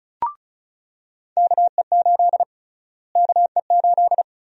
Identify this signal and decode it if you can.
Morse code, heard as a keyed tone, about 34 wpm, 700 Hz. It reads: KE8 KE8
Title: USA Call Sign Prefixes - Repeated 2x - 35wpm